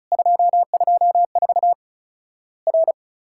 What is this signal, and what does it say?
Morse code, heard as a keyed tone, about 34 wpm, 700 Hz. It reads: 224 R
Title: Worked All Europe DX Contest - QTC Exchange - 35wpm